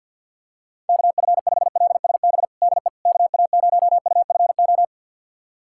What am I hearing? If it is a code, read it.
Morse code, heard as a keyed tone, about 52 wpm, 700 Hz. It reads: KF5LSB DE KA0UFO
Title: USA Callsign DE Callsign 50wpm